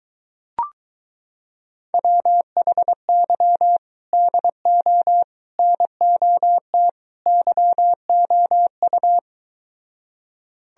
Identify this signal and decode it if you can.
Morse code, heard as a keyed tone, about 22 wpm, 700 Hz. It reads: WHY DO NOT YOU